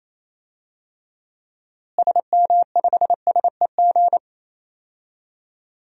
Morse code, heard as a keyed tone, about 28 wpm, 700 Hz. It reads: SM5SEG
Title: DXCC-100 Call Signs - 6 Characters - Repeated 3x - 28wpm